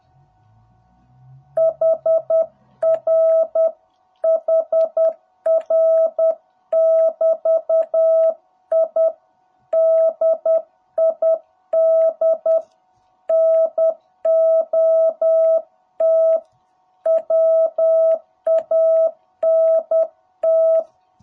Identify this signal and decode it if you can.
Morse code, heard as a keyed tone, about 10 wpm, 660 Hz. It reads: HR HR <BT> I DID NOT WANT
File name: jh8cbh / 2021cw